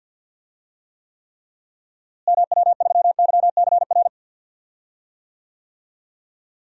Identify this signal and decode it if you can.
Morse code, heard as a keyed tone, about 47 wpm, 700 Hz. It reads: MW3YCR
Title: DXCC-100 Call Signs - 6 Characters - Repeated 3x - 50wpm